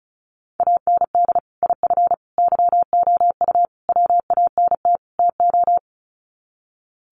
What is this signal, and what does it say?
Morse code, heard as a keyed tone, about 34 wpm, 700 Hz. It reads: AND IF YOU WANT TO
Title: Sets of 5 Words Encore 35wpm